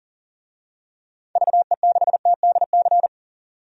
Morse code, heard as a keyed tone, about 39 wpm, 700 Hz. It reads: VE6TDC
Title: DXCC-100 Call Signs - 6 Characters - 40wpm